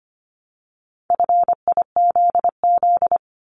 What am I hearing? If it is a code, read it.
Morse code, heard as a keyed tone, about 25 wpm, 700 Hz. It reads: FIZZ
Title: Multi-letter Phonemes Repeated 3x 25wpm